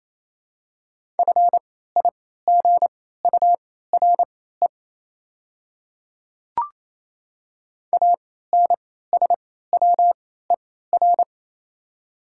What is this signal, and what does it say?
Morse code, heard as a keyed tone, about 28 wpm, spaced as for 16 wpm, 690 Hz. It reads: FIGURE ANSWER
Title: 3x Character Spacing - Top 500 Words - 28wpm